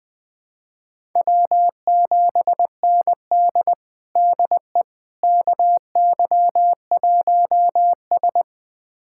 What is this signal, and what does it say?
Morse code, heard as a keyed tone, about 20 wpm, 700 Hz. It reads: W7ND DE KY1S
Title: USA Callsign DE Callsign - 4 Characters - 20wpm